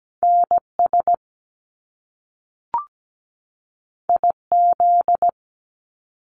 Morse code, heard as a keyed tone, about 17 wpm, 700 Hz. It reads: NS IZ
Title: Top Two Letter Combinations in Words 17wpm